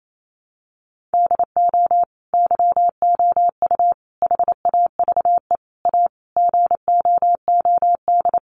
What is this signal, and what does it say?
Morse code, heard as a keyed tone, about 28 wpm, 700 Hz. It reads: DO YOU HAVE A GOOD
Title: Sentences from Top 1300 Words 28wpm